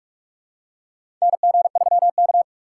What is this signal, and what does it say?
Morse code, heard as a keyed tone, about 42 wpm, 700 Hz. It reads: NG3K
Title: USA Call Signs - 4 Characters - 3x Character Spacing - 45wpm